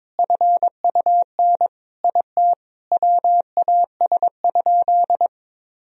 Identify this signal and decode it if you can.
Morse code, heard as a keyed tone, about 22 wpm, 700 Hz. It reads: FUN IT WAS?
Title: Sentences from Top 1000 Words - Review Part 4 - 22wpm